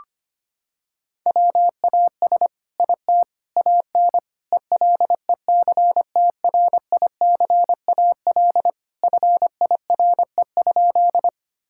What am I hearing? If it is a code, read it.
Morse code, heard as a keyed tone, about 25 wpm, 700 Hz. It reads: WAS IT AN ELECTRICAL FIRE?
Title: Sentences from Top 1000 Words - Review Part 6 - 25wpm